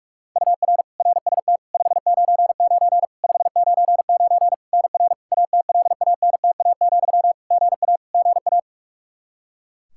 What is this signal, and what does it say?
Morse code, heard as a keyed tone, about 46 wpm, 700 Hz. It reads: UR RST 599 599 NR ATLANTA, GA GA
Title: Practice QSO - Exchange 1 - Beginning - 45wpm